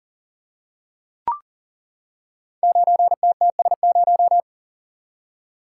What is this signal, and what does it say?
Morse code, heard as a keyed tone, about 40 wpm, 700 Hz. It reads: 9TTS0
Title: Code Groups - 5 Letter:Numbers - 40wpm